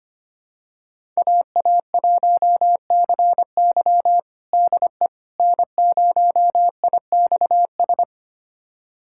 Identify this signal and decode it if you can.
Morse code, heard as a keyed tone, about 25 wpm, 700 Hz. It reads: AA1CY DE N0IXS
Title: USA Callsign DE Callsign - 5 Characters - 25wpm